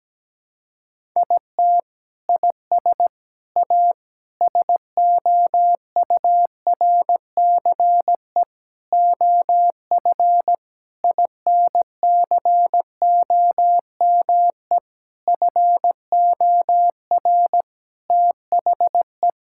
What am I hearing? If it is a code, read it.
Morse code, heard as a keyed tone, about 17 wpm, 700 Hz. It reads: IT IS A SOURCE OF INCOME FOR THE
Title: Sentences from Top 1000 Words - Review Part 5 - 17wpm